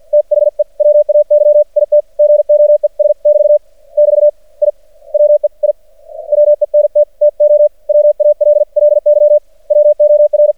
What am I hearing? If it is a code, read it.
Morse code, heard as a keyed tone, about 50 wpm, 590 Hz. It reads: THE WAY IT GOES <BT> <BT> I WE I WENT TO WALLY WOR
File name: archived / CRSnet